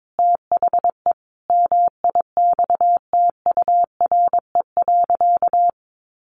Morse code, heard as a keyed tone, about 22 wpm, 700 Hz. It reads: THE MIXTURE.
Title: Sentences from Top 1500 Words 22wpm